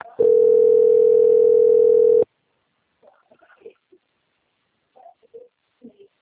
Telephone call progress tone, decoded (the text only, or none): ringback tone